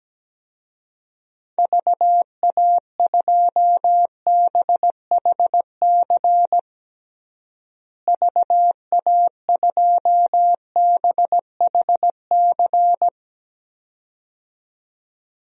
Morse code, heard as a keyed tone, about 17 wpm, 700 Hz. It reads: VA2BHC VA2BHC
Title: DXCC-100 Call Signs - 6 Characters - Repeated 2x - 17wpm